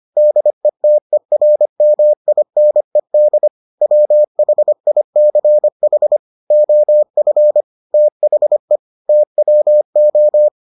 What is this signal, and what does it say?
Morse code, heard as a keyed tone, about 25 wpm, 600 Hz. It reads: DETERMINED WHICH OF THE TWO